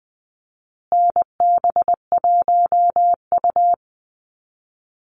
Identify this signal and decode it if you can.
Morse code, heard as a keyed tone, about 19 wpm, 700 Hz. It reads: NB1U
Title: Call Signs Repeated 2x - 20wpm